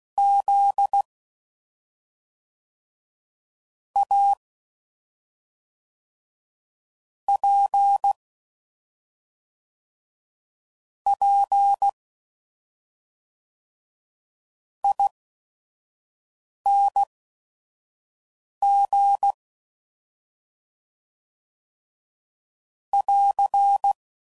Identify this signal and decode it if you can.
Morse code, heard as a keyed tone, about 16 wpm, spaced as for 5 wpm, 800 Hz. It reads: Z A P P ING <AR>
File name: Suffixes / Special ING's / SlowToFaster ING's